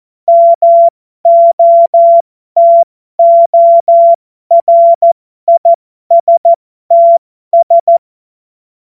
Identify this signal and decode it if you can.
Morse code, heard as a keyed tone, about 14 wpm, spaced as for 12 wpm, 680 Hz. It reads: MOTORISTS
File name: Morsekurs / Witze